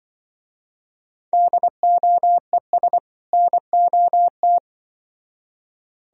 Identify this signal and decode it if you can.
Morse code, heard as a keyed tone, about 24 wpm, 700 Hz. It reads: DOES NOT
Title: Speed-Racing - Sets of 2 Words 20wpm